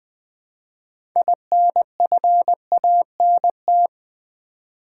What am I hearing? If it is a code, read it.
Morse code, heard as a keyed tone, about 20 wpm, 700 Hz. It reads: INFANT
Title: Build A Word - 2 Letters At A Time - 6-Letter Words - 20wpm